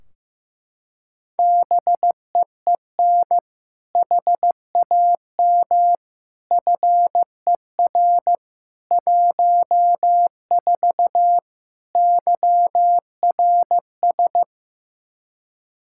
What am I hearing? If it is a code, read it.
Morse code, heard as a keyed tone, about 15 wpm, 700 Hz. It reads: BEEN HAM FER 14 YRS
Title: Practice QSO - Exchange 3 - More Info - 15wpm